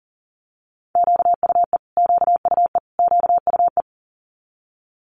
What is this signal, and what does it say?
Morse code, heard as a keyed tone, about 42 wpm, 700 Hz. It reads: QUE QUE QUE
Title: Multi-letter Phonemes Repeated 3x 40wpm